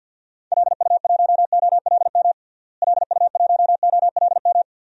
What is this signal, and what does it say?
Morse code, heard as a keyed tone, about 52 wpm, 700 Hz. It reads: LU1OLM LU1OLM
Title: DXCC-100 Call Signs - 6 Characters - Repeated 3x - 50wpm